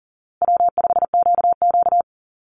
Morse code, heard as a keyed tone, about 40 wpm, 700 Hz. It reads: W5QQ
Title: Easy Call Signs Repeated 3x - 40wpm